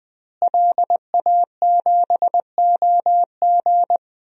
Morse code, heard as a keyed tone, about 20 wpm, 700 Hz. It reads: LA7OG
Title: DXCC-100 Call Signs - 5 Characters - Repeated 3x - 20wpm